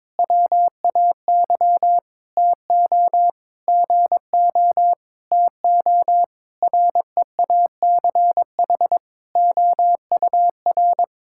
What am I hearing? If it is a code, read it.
Morse code, heard as a keyed tone, about 22 wpm, 700 Hz. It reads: WAY TO GO TO REACH OUR
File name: Sentences with 7-Word Phrases - 22wpm